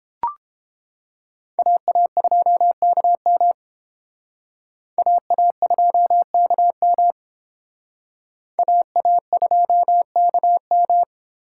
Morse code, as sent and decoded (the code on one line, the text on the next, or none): .- .- ..--- -.- -- / .- .- ..--- -.- -- / .- .- ..--- -.- --
AA2KM AA2KM AA2KM